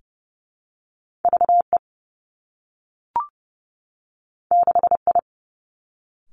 Morse code, heard as a keyed tone, about 32 wpm, 700 Hz. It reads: V E 6 I
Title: Two Challenging Letter-Number Combinations 30wpm